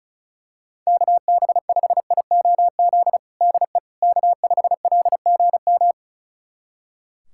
Morse code, heard as a keyed tone, about 36 wpm, 700 Hz. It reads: KB5IOZ DE K5LGM
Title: USA Callsign DE Callsign 35wpm